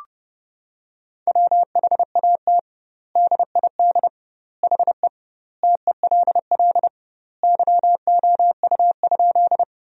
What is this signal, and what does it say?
Morse code, heard as a keyed tone, about 30 wpm, 700 Hz. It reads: WHAT DID HE TELL YOU?
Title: 2x Word-Spacing - Sentences from Top 200 Words - 30wpm